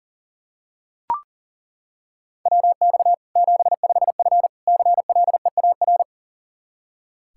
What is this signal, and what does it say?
Morse code, heard as a keyed tone, about 40 wpm, 700 Hz. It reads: WX 75F CLEAR
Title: Practice QSO - Exchange 2 - Info - 40wpm